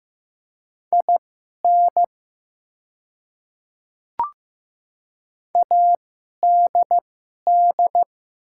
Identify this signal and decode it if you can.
Morse code, heard as a keyed tone, about 15 wpm, spaced as for 11 wpm, 700 Hz. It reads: IN ADD
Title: 2x Character Spacing - Top 500 Words - 15wpm